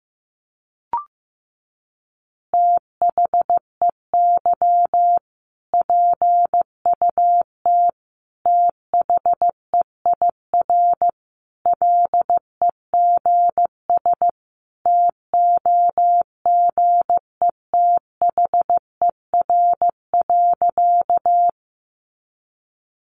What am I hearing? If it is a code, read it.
Morse code, heard as a keyed tone, about 15 wpm, 700 Hz. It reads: THEY PUT THEIR LEGS TOGETHER.
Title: Easy Phrases 15wpm